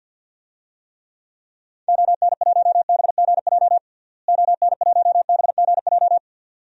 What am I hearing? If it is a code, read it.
Morse code, heard as a keyed tone, about 50 wpm, 700 Hz. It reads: ON1BGJ ON1BGJ
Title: DXCC-100 Call Signs - 6 Characters - Repeated 2x - 50wpm